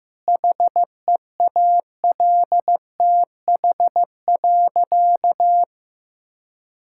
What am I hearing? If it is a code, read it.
Morse code, heard as a keyed tone, about 15 wpm, 700 Hz. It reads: HEALTH.
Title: Sentences from Top 1000 Words - Review Part 5 - 15wpm